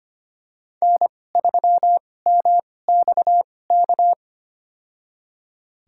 Morse code, heard as a keyed tone, about 25 wpm, spaced as for 18 wpm, 700 Hz. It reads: N3MXK
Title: USA Call Signs - 5 Characters - 2x Character Spacing - 25wpm